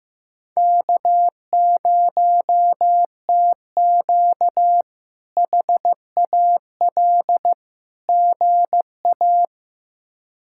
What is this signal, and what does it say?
Morse code, heard as a keyed tone, about 15 wpm, 700 Hz. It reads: K0TQ HAL GA
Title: K1USN SST QSO Practice - 15wpm